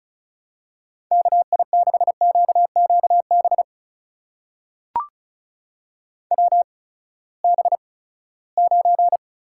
Morse code, heard as a keyed tone, about 36 wpm, spaced as for 20 wpm, 700 Hz. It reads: KI6QQB WB9
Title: USA Call Signs - 6 Characters - 8x Character Spacing - 35wpm